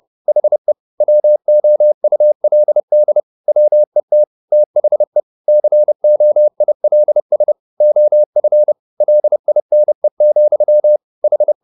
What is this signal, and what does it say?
Morse code, heard as a keyed tone, about 30 wpm, 600 Hz. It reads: HE WOULD WET THE COILS OF LINE, H